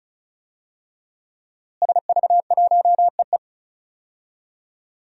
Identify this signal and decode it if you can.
Morse code, heard as a keyed tone, about 34 wpm, 700 Hz. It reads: SV1EE
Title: DXCC-100 Call Signs - 5 Characters - Repeated 2x - 35wpm